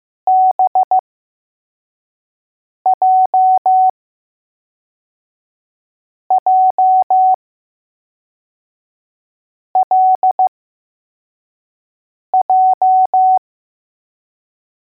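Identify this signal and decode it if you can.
Morse code, heard as a keyed tone, about 15 wpm, spaced as for 4 wpm, 750 Hz. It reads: BJJLJ
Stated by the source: Morse code letters test drill for L, J and B.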